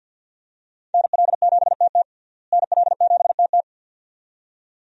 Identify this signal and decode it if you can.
Morse code, heard as a keyed tone, about 51 wpm, 700 Hz. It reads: NL7TT NL7TT
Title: USA Call Signs - 5 Characters - Repeated 2x - 50wpm